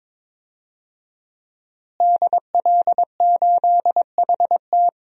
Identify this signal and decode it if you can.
Morse code, heard as a keyed tone, about 22 wpm, 700 Hz. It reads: DL8HT
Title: DXCC-100 Call Signs - 5 Characters - Repeated 3x - 22wpm